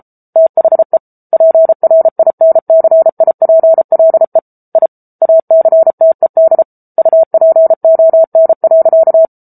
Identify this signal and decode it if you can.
Morse code, heard as a keyed tone, about 34 wpm, 650 Hz. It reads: THE PRINCIPLE I ACTED UPON.